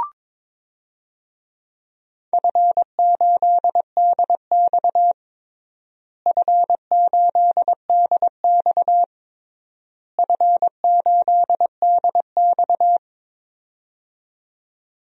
Morse code, heard as a keyed tone, about 22 wpm, 700 Hz. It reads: F8DX F8DX F8DX
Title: DXCC-100 Call Signs - 4 Characters - Repeated 3x - 22wpm